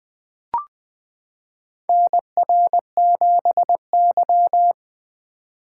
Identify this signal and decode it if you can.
Morse code, heard as a keyed tone, about 20 wpm, 700 Hz. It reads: NR7Y
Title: 2023 SOTA Chaser Call Signs - 20wpm